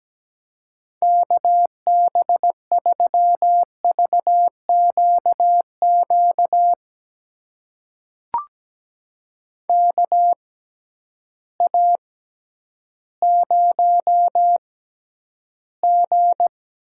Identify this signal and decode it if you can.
Morse code, heard as a keyed tone, about 17 wpm, spaced as for 10 wpm, 700 Hz. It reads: KB3VQQ KA0G
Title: USA Call Signs - 6 Characters - 6x Character Spacing - 17wpm